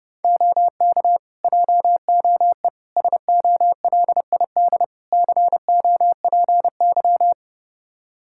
Morse code, heard as a keyed tone, about 30 wpm, 700 Hz. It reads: OK JOE SOLID COPY
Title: Practice QSO - Exchange 3 - More Info - 30wpm